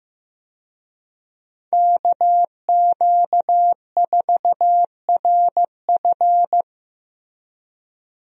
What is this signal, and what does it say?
Morse code, heard as a keyed tone, about 15 wpm, 700 Hz. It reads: KQ4RF
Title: USA Call Signs - 5 Characters - 15wpm